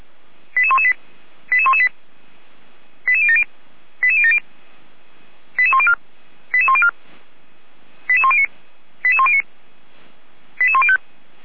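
Five-tone selective call (ZVEI), heard as quick runs of five tones, tones of about 70 ms. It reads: 88118 88118 88070 88070 88104 88104 88109 88109 88105
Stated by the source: ZVEI five-tone selective call code 88192, received by SDR, from fire brigade Feuerwehr Orlamünde's audio server.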